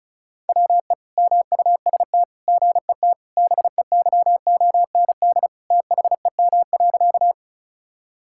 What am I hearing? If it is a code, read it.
Morse code, heard as a keyed tone, about 36 wpm, 700 Hz. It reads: WE MUST GET BEYOND THEM.
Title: Sentences from Top 1000 Words - Review Part 6 - 35wpm